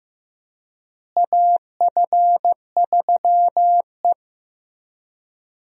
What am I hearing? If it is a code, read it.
Morse code, heard as a keyed tone, about 16 wpm, 700 Hz. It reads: AF3E
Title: USA Call Signs - 4 Characters - Repeated 3x - 15wpm